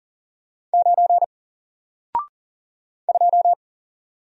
Morse code, heard as a keyed tone, about 40 wpm, 700 Hz. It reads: 9 2
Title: Single Letter-Number Alt 40wpm